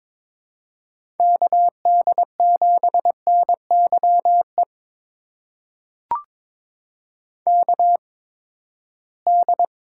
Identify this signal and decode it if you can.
Morse code, heard as a keyed tone, about 21 wpm, spaced as for 15 wpm, 700 Hz. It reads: KD7NYE KD
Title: USA Call Signs - 6 Characters - 8x Character Spacing - 22wpm